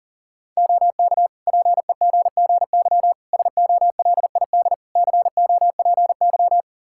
Morse code, heard as a keyed tone, about 40 wpm, 700 Hz. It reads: OK PEGGY SOLID COPY
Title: Practice QSO - Exchange 3 - More Info - 40wpm